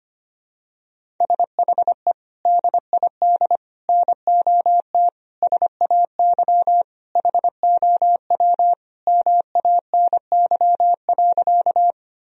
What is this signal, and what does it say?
Morse code, heard as a keyed tone, about 25 wpm, 700 Hz. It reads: SHE DID NOT SAY HOW MANY.
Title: Sentences from Top 1000 Words - Review Part 1 - 25wpm